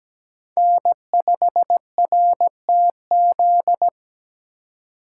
Morse code, heard as a keyed tone, about 17 wpm, 700 Hz. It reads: N5RTZ
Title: USA Call Signs - 2x Character Spacing - 17wpm